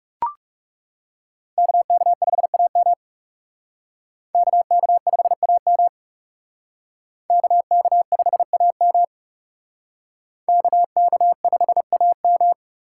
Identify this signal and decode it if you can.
Morse code, heard as a keyed tone, about 37 wpm, 700 Hz. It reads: KK5AM KK5AM KK5AM KK5AM
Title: Speed Racing - USA Call Signs 30wpm